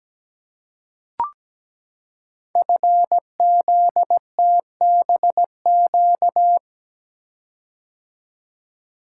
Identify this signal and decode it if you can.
Morse code, heard as a keyed tone, about 17 wpm, 700 Hz. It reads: FZTBQ
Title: Code Groups - 5 Letters - 17wpm